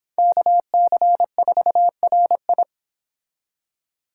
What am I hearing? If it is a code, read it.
Morse code, heard as a keyed tone, about 29 wpm, 700 Hz. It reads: KC4RI